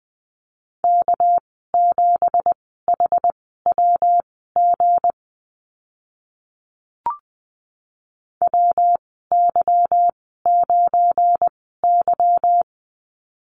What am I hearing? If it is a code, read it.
Morse code, heard as a keyed tone, about 20 wpm, spaced as for 14 wpm, 700 Hz. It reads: K7HWG WY9Y
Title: 238 Lesson 38g - Letter q - Call Signs 20wpm